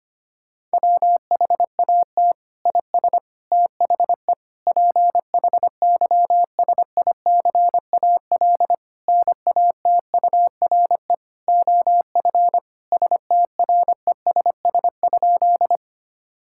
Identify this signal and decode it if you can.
Morse code, heard as a keyed tone, about 25 wpm, 700 Hz. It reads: WHAT IS THE PHYSICAL NATURE OF STRESS?
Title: Sentences from Top 1000 Words - Review Part 6 - 25wpm